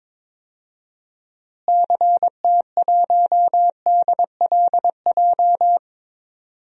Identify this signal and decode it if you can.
Morse code, heard as a keyed tone, about 22 wpm, 700 Hz. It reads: CT1DLJ
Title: DXCC-100 Call Signs - 6 Characters - Repeated 3x - 22wpm